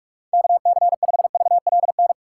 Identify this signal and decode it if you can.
Morse code, heard as a keyed tone, about 46 wpm, 700 Hz. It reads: KC5VLN
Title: USA Call Signs - 2x Character Spacing - 45wpm